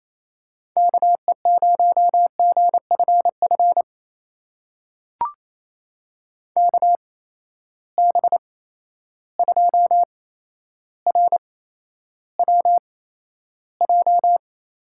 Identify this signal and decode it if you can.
Morse code, heard as a keyed tone, about 28 wpm, spaced as for 12 wpm, 700 Hz. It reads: KE0GFF KB2RWJ
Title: USA Call Signs - 6 Characters - 8x Character Spacing - 28wpm